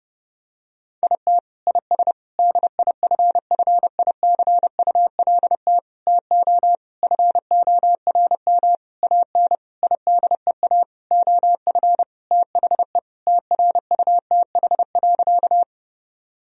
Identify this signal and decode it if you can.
Morse code, heard as a keyed tone, about 30 wpm, 700 Hz. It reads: IT IS DIFFICULT TO FORM AN IDEA OF THE TRUTH.
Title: Sentences from Top 1000 Words - Review Part 1 - 30wpm